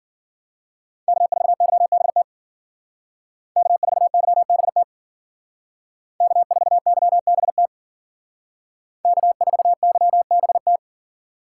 Morse code, heard as a keyed tone, about 51 wpm, 700 Hz. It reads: K4YBT K4YBT K4YBT K4YBT